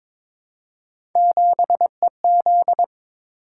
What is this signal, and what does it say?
Morse code, heard as a keyed tone, about 22 wpm, 700 Hz. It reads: 7EZ
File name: practice / 22wpm / Three Letter-Number Combinations - 22wpm